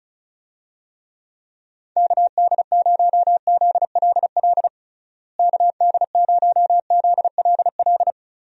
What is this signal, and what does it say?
Morse code, heard as a keyed tone, about 36 wpm, 700 Hz. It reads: KD0ZLL KD0ZLL
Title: USA Call Signs - 6 Characters - Repeated 2x - 35wpm